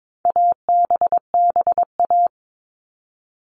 Morse code, heard as a keyed tone, about 22 wpm, 700 Hz. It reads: ABBA
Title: All Cut Numbers - 4-Digit 22wpm